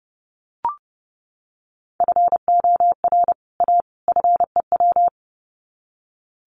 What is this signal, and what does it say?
Morse code, heard as a keyed tone, about 30 wpm, 700 Hz. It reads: FOR A FEW
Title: Sets of 3 Words 30wpm